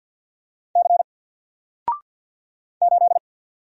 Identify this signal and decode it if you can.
Morse code, heard as a keyed tone, about 50 wpm, 700 Hz. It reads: C 8
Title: Single Letter-Number 50wpm